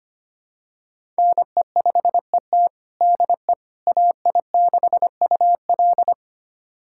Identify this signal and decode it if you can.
Morse code, heard as a keyed tone, about 25 wpm, 700 Hz. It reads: NE5ET DE AI6UL
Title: USA Callsign DE Callsign - 5 Characters - 25wpm